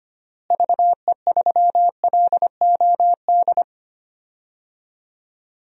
Morse code, heard as a keyed tone, about 25 wpm, 700 Hz. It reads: VE3LOD
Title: DXCC-100 Call Signs - 6 Characters - Repeated 2x - 25wpm